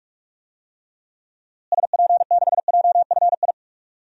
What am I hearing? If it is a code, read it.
Morse code, heard as a keyed tone, about 46 wpm, 700 Hz. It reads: SP6JFI
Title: DXCC-100 Call Signs - 6 Characters - Repeated 2x - 45wpm